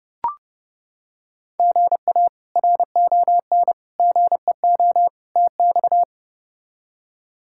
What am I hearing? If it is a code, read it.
Morse code, heard as a keyed tone, about 32 wpm, 700 Hz. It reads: GA RON GEO TX
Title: K1USN SST QSO Practice - 30wpm